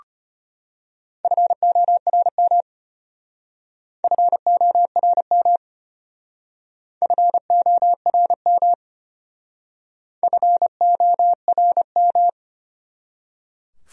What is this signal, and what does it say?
Morse code, heard as a keyed tone, about 32 wpm, 700 Hz. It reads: FORM FORM FORM FORM